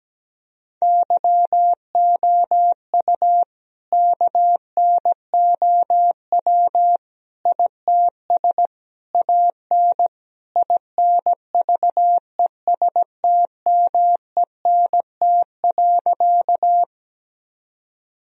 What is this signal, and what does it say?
Morse code, heard as a keyed tone, about 17 wpm, 700 Hz. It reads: YOU KNOW ITS AN INVESTMENT.
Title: Sentences from Top 1000 Words - Review Part 4 - 17wpm